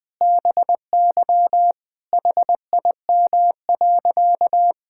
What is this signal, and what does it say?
Morse code, heard as a keyed tone, about 21 wpm, 700 Hz. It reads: BY HIM.
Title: Sentences from Top 1000 Words - Review Part 6 - 20wpm